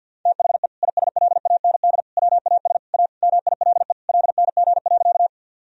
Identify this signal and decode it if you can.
Morse code, heard as a keyed tone, about 49 wpm, 700 Hz. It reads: THE ISLAND WAS A MILE LNG.
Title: Sentences from Top 500 Words and Abbreviations 50wpm